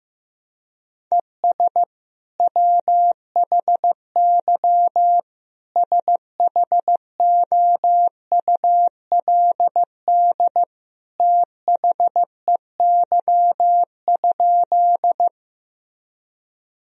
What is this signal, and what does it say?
Morse code, heard as a keyed tone, about 15 wpm, 700 Hz. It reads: ES WHY SHOULD THEY?
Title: Sentences from Top 300 Words and Abbreviations 15wpm